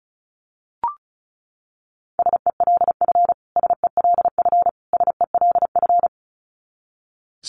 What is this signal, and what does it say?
Morse code, heard as a keyed tone, about 34 wpm, 700 Hz. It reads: SELF SELF SELF
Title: Top 200 Words Repeated 35wpm